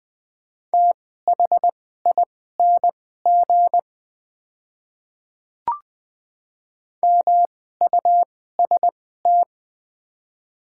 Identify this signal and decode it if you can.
Morse code, heard as a keyed tone, about 20 wpm, spaced as for 15 wpm, 700 Hz. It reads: THING MUST
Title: 2x Character Spacing - Top 200 Words - 20wpm